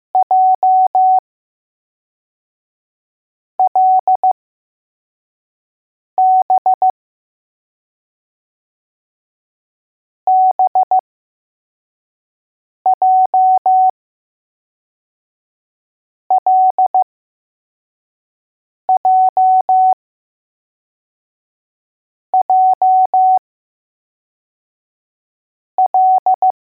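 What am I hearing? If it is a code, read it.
Morse code, heard as a keyed tone, about 15 wpm, spaced as for 4 wpm, 750 Hz. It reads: JLB BJLJJL
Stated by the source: Morse code letters test drill for L, J and B.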